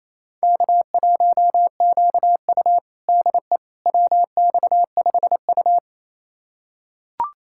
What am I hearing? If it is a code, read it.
Morse code, heard as a keyed tone, about 28 wpm, 700 Hz. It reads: K1QU DE WX5U
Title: USA Callsign DE Callsign - 4 Characters - 28wpm